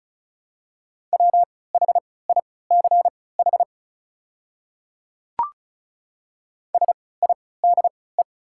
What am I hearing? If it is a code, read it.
Morse code, heard as a keyed tone, about 34 wpm, spaced as for 20 wpm, 690 Hz. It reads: WHICH SIDE
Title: 3x Character Spacing - Top 100 Words - 35wpm